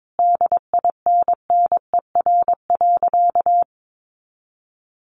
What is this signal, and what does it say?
Morse code, heard as a keyed tone, about 22 wpm, 700 Hz. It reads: DINNER.
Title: Easy Phrases 22wpm